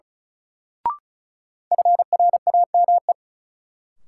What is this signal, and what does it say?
Morse code, heard as a keyed tone, about 36 wpm, 700 Hz. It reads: FRAME
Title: Sentences from Top 1400 Words - New Word Drill Alt - 35wpm